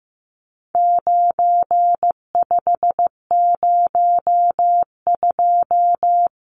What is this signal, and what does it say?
Morse code, heard as a keyed tone, about 15 wpm, 700 Hz. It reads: 9502
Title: Numbers - 4-Digit 15wpm